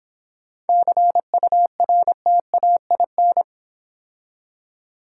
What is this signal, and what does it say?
Morse code, heard as a keyed tone, about 26 wpm, 700 Hz. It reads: CURTAIN